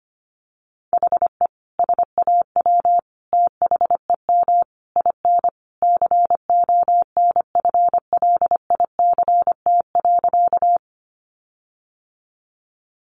Morse code, heard as a keyed tone, about 25 wpm, 700 Hz. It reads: HE SAW THEM IN CONFLICT.
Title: Sentences from Top 1300 Words 25wpm